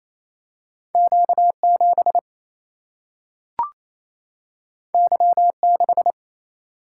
Morse code, heard as a keyed tone, about 28 wpm, 700 Hz. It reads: Q7 Y6
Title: Single-Letter Single-Number 28wpm